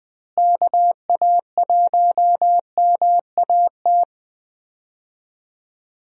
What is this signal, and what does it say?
Morse code, heard as a keyed tone, about 20 wpm, 700 Hz. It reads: KA1MAT
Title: USA Call Signs - 6 Characters - Repeated 3x - 20wpm